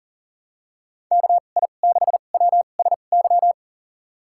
Morse code, heard as a keyed tone, about 40 wpm, spaced as for 29 wpm, 700 Hz. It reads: KI6WSY
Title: USA Call Signs - 6 Characters - 2x Character Spacing - 40wpm